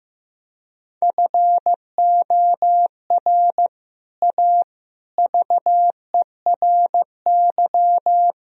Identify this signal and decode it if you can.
Morse code, heard as a keyed tone, about 15 wpm, 700 Hz. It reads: FOR A VERY
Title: Speed-Racing - Sets of 5 Words 15wpm